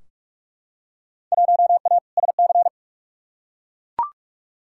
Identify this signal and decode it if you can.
Morse code, heard as a keyed tone, about 43 wpm, 700 Hz. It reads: 1A SC
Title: Field Day Exchanges - 45wpm